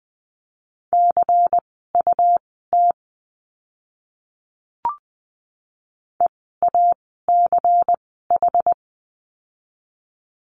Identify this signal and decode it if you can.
Morse code, heard as a keyed tone, about 20 wpm, spaced as for 15 wpm, 700 Hz. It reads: CUT EACH
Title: 101 Lesson 17f - Letter u - Words 20wpm